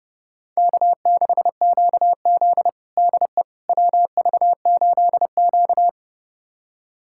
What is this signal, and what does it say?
Morse code, heard as a keyed tone, about 32 wpm, 700 Hz. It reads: K6QZ DE WV8Q
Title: USA Callsign DE Callsign - 4 Characters - 30wpm